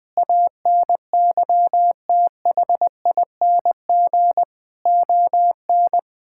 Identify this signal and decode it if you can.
Morse code, heard as a keyed tone, about 21 wpm, 700 Hz. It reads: ANYTHING ON
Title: Sentences from Top 1000 Words - Review Part 5 - 20wpm